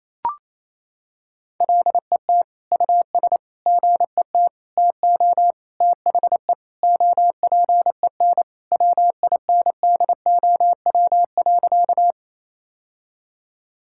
Morse code, heard as a keyed tone, about 28 wpm, 700 Hz. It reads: LET US GET TO THE OPEN WINDOW.
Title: Sentences from Top 1600 Words 28wpm